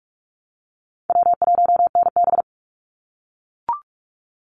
Morse code, heard as a keyed tone, about 44 wpm, 700 Hz. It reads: W1NB
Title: USA Call Signs 45wpm